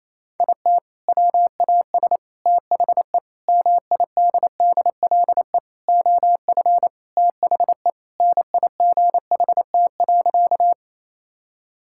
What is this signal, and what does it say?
Morse code, heard as a keyed tone, about 28 wpm, 700 Hz. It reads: IT WAS THE MIDDLE OF THE NIGHT.
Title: Sentences from Top 1000 Words - Review Part 6 - 28wpm